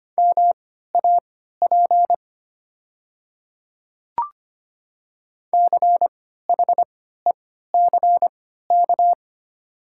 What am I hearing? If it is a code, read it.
Morse code, heard as a keyed tone, about 25 wpm, spaced as for 14 wpm, 700 Hz. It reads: MAP CHECK
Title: 3x Character Spacing - Top 500 Words - 25wpm